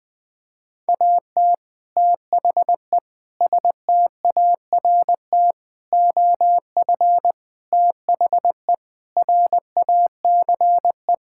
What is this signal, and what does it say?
Morse code, heard as a keyed tone, about 21 wpm, 700 Hz. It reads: AT THE START OF THE RACE
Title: Sentences with 5-Word Phrases - 20wpm